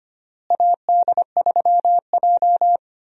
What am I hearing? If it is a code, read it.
Morse code, heard as a keyed tone, about 25 wpm, 700 Hz. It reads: AD3J
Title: USA Call Signs - 4 Characters - Repeated 3x - 25wpm